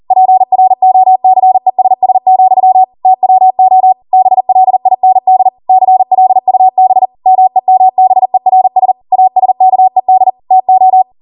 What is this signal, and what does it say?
Morse code, heard as a keyed tone, about 40 wpm, 750 Hz. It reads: PROCESS, TWO BLIND CLUB MEMBERS ASKED TO